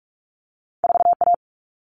Morse code, heard as a keyed tone, about 46 wpm, 700 Hz. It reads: 4A
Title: Common Cut Numbers - 2-Digit 45wpm